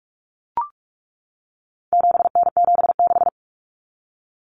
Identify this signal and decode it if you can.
Morse code, heard as a keyed tone, about 44 wpm, 700 Hz. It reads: 7N76
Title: Common Cut Numbers - 4-Digit 45wpm